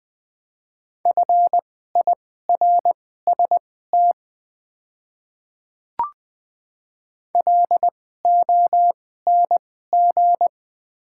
Morse code, heard as a keyed tone, about 20 wpm, spaced as for 14 wpm, 700 Hz. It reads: FIRST LONG
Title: 2x Character Spacing - Top 100 Words - 20wpm